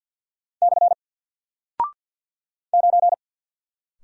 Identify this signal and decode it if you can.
Morse code, heard as a keyed tone, about 48 wpm, 700 Hz. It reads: / 9
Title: Single Character (Letter:Number:Punctuation:Prosign) Alt 50wpm